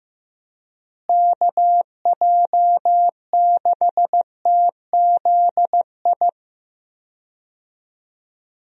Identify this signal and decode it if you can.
Morse code, heard as a keyed tone, about 15 wpm, 700 Hz. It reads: KJ6TZI
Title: USA Call Signs - 6 Characters - Repeated 2x - 15wpm